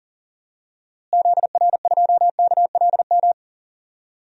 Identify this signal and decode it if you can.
Morse code, heard as a keyed tone, about 40 wpm, 700 Hz. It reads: ZR2KLM
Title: DXCC-100 Call Signs - 6 Characters - Repeated 2x - 40wpm